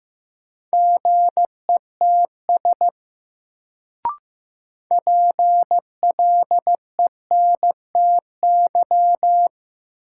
Morse code, heard as a keyed tone, about 15 wpm, 700 Hz. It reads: GETS PLENTY
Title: Sentences from Top 1300 Words - New Word Drill - 15wpm